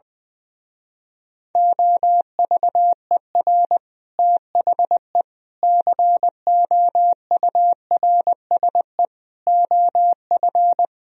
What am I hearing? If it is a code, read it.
Morse code, heard as a keyed tone, about 21 wpm, 700 Hz. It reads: OVER THE COURSE OF
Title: Speed-Racing - Sets of 5 Words 20wpm